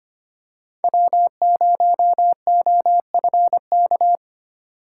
Morse code, heard as a keyed tone, about 25 wpm, 700 Hz. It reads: W0OFK
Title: USA Call Signs - 5 Characters - 4x Character Spacing - 25wpm